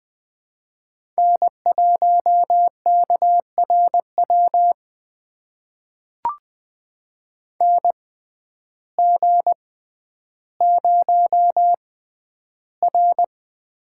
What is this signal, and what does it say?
Morse code, heard as a keyed tone, about 20 wpm, spaced as for 11 wpm, 700 Hz. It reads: N1KRW NG0R
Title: USA Call Signs - 6x Character Spacing - 20wpm